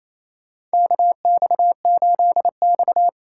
Morse code, heard as a keyed tone, about 28 wpm, 700 Hz. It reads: KX8X
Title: K1USN SST QSO Practice - 28wpm